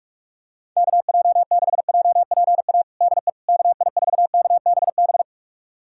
Morse code, heard as a keyed tone, about 42 wpm, 700 Hz. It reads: KJ6JPA DE KI4KBB
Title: USA Callsign DE Callsign - 6 Characters - 45wpm